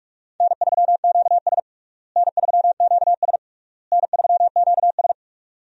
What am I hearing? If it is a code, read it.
Morse code, heard as a keyed tone, about 46 wpm, 700 Hz. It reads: N3QS N3QS N3QS
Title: USA Call Signs - 4 Characters - Repeated 3x - 45wpm